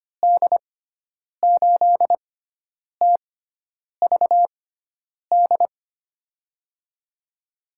Morse code, heard as a keyed tone, about 25 wpm, spaced as for 9 wpm, 700 Hz. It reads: D8TVD
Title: USA Call Signs - 6 Characters - 6x Character Spacing - 25wpm